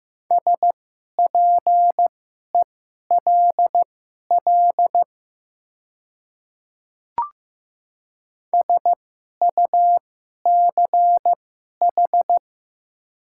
Morse code, heard as a keyed tone, about 15 wpm, spaced as for 11 wpm, 700 Hz. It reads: SPELL SUCH
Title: 2x Character Spacing - Top 200 Words - 15wpm